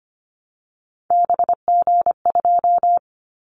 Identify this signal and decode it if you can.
Morse code, heard as a keyed tone, about 25 wpm, 700 Hz. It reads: BG2
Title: Callsign Prefixes for Top 100 DXCC - 25wpm